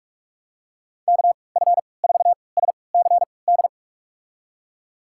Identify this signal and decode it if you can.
Morse code, heard as a keyed tone, about 47 wpm, spaced as for 26 wpm, 700 Hz. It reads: KF4SCD
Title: USA Call Signs - 6 Characters - 3x Character Spacing - 45wpm